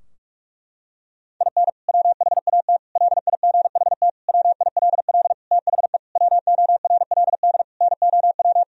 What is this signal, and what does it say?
Morse code, heard as a keyed tone, about 46 wpm, 700 Hz. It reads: IN WHAT LIGHT WILL THE WORLD NOW
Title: Sentences from Top 1000 Words - Review Part 2 - 45wpm